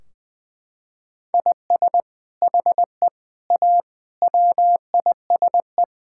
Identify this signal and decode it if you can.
Morse code, heard as a keyed tone, about 20 wpm, 700 Hz. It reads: IS HE A WISE
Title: Sentences from Top 1600 Words 20wpm